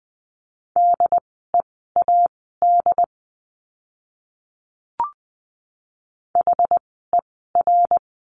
Morse code, heard as a keyed tone, about 20 wpm, spaced as for 14 wpm, 700 Hz. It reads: DEAD HER
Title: 075 Lesson 13f - Letter L - Words 20wpm